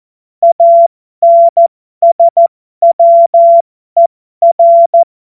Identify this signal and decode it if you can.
Morse code, heard as a keyed tone, about 14 wpm, spaced as for 12 wpm, 680 Hz. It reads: ANSWER
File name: Morsekurs / Witze